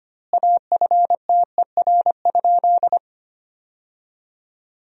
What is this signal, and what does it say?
Morse code, heard as a keyed tone, about 25 wpm, 700 Hz. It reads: AFTER?
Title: 2x Word-Spacing - Sentences from Top 200 Words - 25wpm